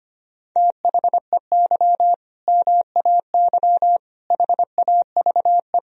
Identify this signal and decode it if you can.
Morse code, heard as a keyed tone, about 25 wpm, 700 Hz. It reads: THEY MAY HAVE
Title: Sentences from Top 1000 Words - Review Part 2 - 25wpm